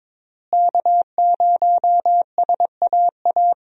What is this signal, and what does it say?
Morse code, heard as a keyed tone, about 22 wpm, 700 Hz. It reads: K0SAA
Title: USA Call Signs - 5 Characters - Repeated 3x - 22wpm